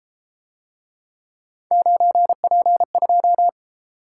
Morse code, heard as a keyed tone, about 30 wpm, 700 Hz. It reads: 9P2